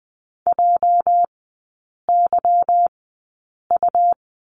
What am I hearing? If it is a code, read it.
Morse code, heard as a keyed tone, about 20 wpm, spaced as for 8 wpm, 690 Hz. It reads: JYU
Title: 205 Lesson 33e - Letter j - Three Characters 20wpm